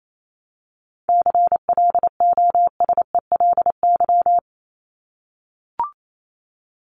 Morse code, heard as a keyed tone, about 28 wpm, 700 Hz. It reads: CLOSELY
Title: Top 1251-1500 Words 28wpm